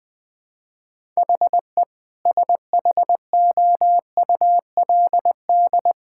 Morse code, heard as a keyed tone, about 21 wpm, 700 Hz. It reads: HE SHOULD
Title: Sentences from Top 1600 Words 20wpm